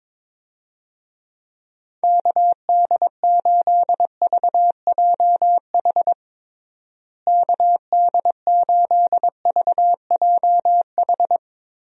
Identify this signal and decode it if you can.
Morse code, heard as a keyed tone, about 22 wpm, 700 Hz. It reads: KD8VJH KD8VJH
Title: USA Call Signs - 6 Characters - Repeated 2x - 22wpm